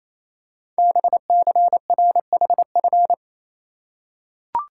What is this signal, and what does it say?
Morse code, heard as a keyed tone, about 28 wpm, 700 Hz. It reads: BCRHF
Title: Code Groups - 5 Letters - 28wpm